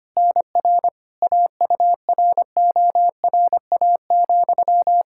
Morse code, heard as a keyed tone, about 25 wpm, 700 Hz. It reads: NR AURORA,
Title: K1USN SST QSO Practice 25wpm